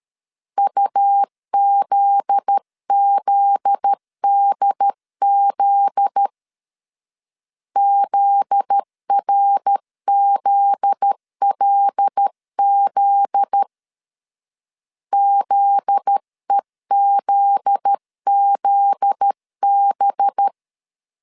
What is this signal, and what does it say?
Morse code, heard as a keyed tone, about 13 wpm, 800 Hz. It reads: UZZDZ ZRZLZ ZEZZB
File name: LEZIONI / 7 GF9Z / 7 GF9Z 12wpm 2147